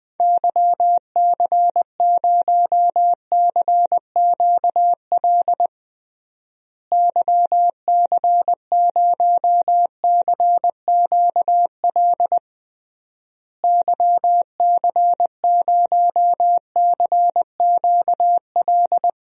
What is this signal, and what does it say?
Morse code, heard as a keyed tone, about 20 wpm, 700 Hz. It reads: YC0CQL YC0CQL YC0CQL
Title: DXCC-100 Call Signs - 6 Characters - Repeated 3x - 20wpm